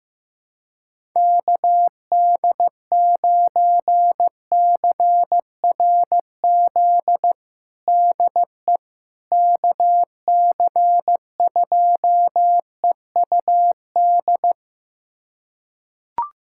Morse code, heard as a keyed tone, about 15 wpm, 700 Hz. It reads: KD9CRZ DE KC2EUD
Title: USA Callsign DE Callsign 15wpm